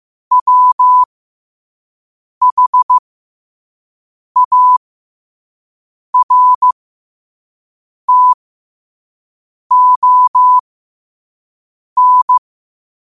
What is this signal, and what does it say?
Morse code, heard as a keyed tone, about 15 wpm, spaced as for 5 wpm, 1000 Hz. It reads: WHARTON
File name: qotdfiles / 5wpm